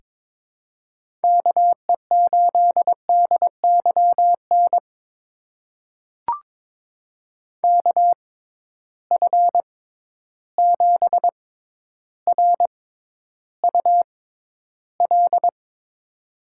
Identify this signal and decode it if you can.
Morse code, heard as a keyed tone, about 22 wpm, spaced as for 11 wpm, 700 Hz. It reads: KE8DYN KF7RUL
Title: USA Call Signs - 6 Characters - 6x Character Spacing - 22wpm